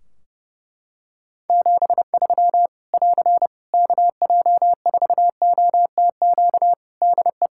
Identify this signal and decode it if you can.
Morse code, heard as a keyed tone, about 30 wpm, 700 Hz. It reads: 73 <AR> KJ4OTQ DE
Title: K1USN SST QSO Practice 30wpm